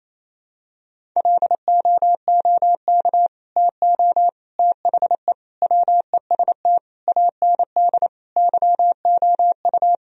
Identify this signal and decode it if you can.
Morse code, heard as a keyed tone, about 27 wpm, 700 Hz. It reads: LOOK TO THE WEST AND YOU
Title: Sentences from Top 1000 Words - Review Part 2 - 28wpm